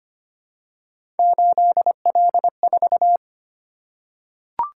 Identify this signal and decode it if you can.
Morse code, heard as a keyed tone, about 25 wpm, 700 Hz. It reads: 8L4
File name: Three Letter-Number Combinations - 25wpm